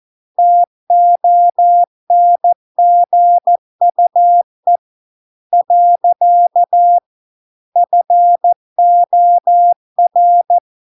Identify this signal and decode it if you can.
Morse code, heard as a keyed tone, about 14 wpm, 700 Hz. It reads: TONGUE . FOR